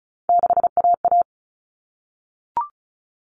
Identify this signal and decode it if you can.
Morse code, heard as a keyed tone, about 34 wpm, 700 Hz. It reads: 6AA
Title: Common Cut Numbers - 3-Digit 35wpm